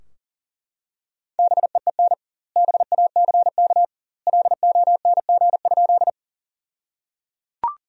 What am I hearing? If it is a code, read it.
Morse code, heard as a keyed tone, about 37 wpm, 700 Hz. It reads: BEEN BACK LONG?
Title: 2x Word-Spacing - Sentences from Top 200 Words - 40wpm